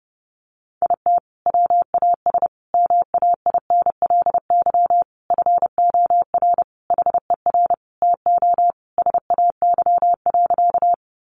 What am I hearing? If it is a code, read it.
Morse code, heard as a keyed tone, about 30 wpm, 700 Hz. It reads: IT WAS MAINLY FOR HER TO SAY.
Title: Sentences from Top 1400 Words 30wpm